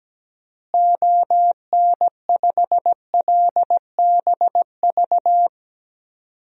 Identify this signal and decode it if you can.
Morse code, heard as a keyed tone, about 17 wpm, 700 Hz. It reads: ON5LBV
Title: DXCC-100 Call Signs - 6 Characters - Repeated 2x - 17wpm